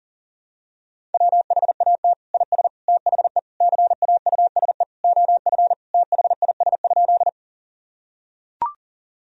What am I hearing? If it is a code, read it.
Morse code, heard as a keyed tone, about 40 wpm, 700 Hz. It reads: WHAT IS THE CAUSE OF THIS?
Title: Sentences from Top 1000 Words - Review Part 1 - 40wpm